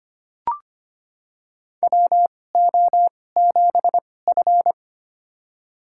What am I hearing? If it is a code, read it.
Morse code, heard as a keyed tone, about 25 wpm, spaced as for 18 wpm, 700 Hz. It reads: WO7F
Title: USA Call Signs - 4 Characters - 2x Character Spacing - 25wpm